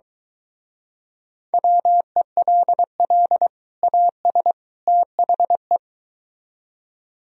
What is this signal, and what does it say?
Morse code, heard as a keyed tone, about 24 wpm, 700 Hz. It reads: WELL AS THE